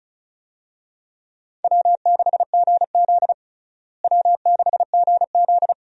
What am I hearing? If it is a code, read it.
Morse code, heard as a keyed tone, about 35 wpm, 700 Hz. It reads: W6GZ W6GZ
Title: USA Call Signs - 4 Characters - Repeated 2x - 35wpm